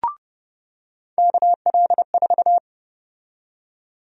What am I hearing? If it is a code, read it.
Morse code, heard as a keyed tone, about 32 wpm, 700 Hz. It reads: KL4
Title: USA Call Sign Prefixes - 30wpm